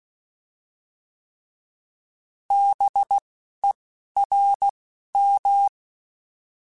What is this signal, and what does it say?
Morse code, heard as a keyed tone, about 16 wpm, 800 Hz. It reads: B E R M